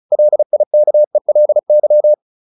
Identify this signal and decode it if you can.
Morse code, heard as a keyed tone, about 34 wpm, 600 Hz. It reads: LIKELY